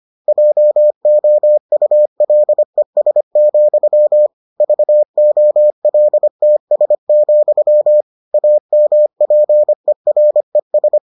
Morse code, heard as a keyed tone, about 25 wpm, 600 Hz. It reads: JOULES, VOLTS, AMPERES